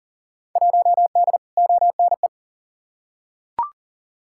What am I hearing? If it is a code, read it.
Morse code, heard as a keyed tone, about 38 wpm, 700 Hz. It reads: 1D ONE
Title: Field Day Exchanges - 40wpm